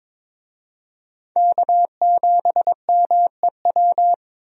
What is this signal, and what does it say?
Morse code, heard as a keyed tone, about 21 wpm, 700 Hz. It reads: K7MEW
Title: USA Call Signs - 5 Characters - 22wpm